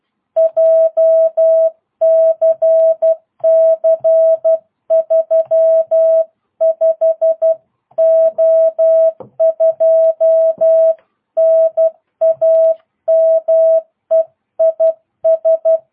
Morse code, heard as a keyed tone, about 12 wpm, 650 Hz. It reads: JCC 35O2 NAME IS